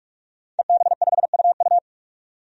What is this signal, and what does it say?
Morse code, heard as a keyed tone, about 46 wpm, 700 Hz. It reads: EB5UU
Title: DXCC-100 Call Signs - 5 Characters - 45wpm